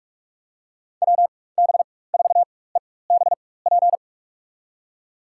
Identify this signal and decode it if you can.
Morse code, heard as a keyed tone, about 46 wpm, spaced as for 21 wpm, 700 Hz. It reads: WB4EBP
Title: USA Call Signs - 6 Characters - 4x Character Spacing - 45wpm